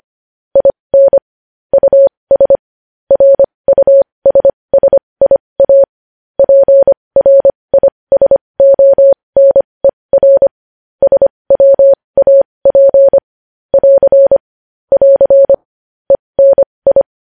Morse code, heard as a keyed tone, about 25 wpm, spaced as for 20 wpm, 550 Hz. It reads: IN US RUSSIA PRISONER SWAP <AR> <AR> ENI
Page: news headlines in morse code at 20 wpm